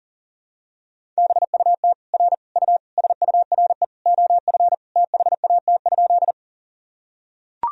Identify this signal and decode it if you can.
Morse code, heard as a keyed tone, about 40 wpm, 700 Hz. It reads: BUT R U SURE OF THAT?
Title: Sentences from Top 300 Words and Abbreviations 40wpm